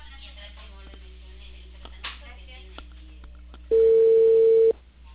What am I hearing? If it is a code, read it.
Ringback tone (China, 450 Hz), with speech.